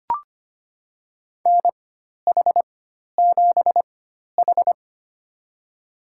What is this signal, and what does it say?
Morse code, heard as a keyed tone, about 25 wpm, spaced as for 12 wpm, 700 Hz. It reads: NH7H
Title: USA Call Signs - 4 Characters - 4x Character Spacing - 25wpm